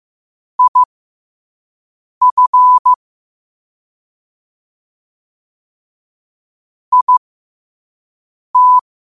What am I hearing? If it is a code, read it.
Morse code, heard as a keyed tone, about 15 wpm, spaced as for 5 wpm, 1000 Hz. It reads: IF IT
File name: qotdfiles / 5wpm